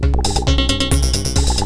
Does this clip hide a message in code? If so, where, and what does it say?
No.